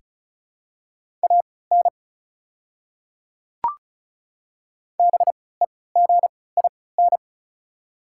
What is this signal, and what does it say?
Morse code, heard as a keyed tone, about 34 wpm, spaced as for 20 wpm, 690 Hz. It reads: AN BEGIN
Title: 3x Character Spacing - Top 500 Words - 35wpm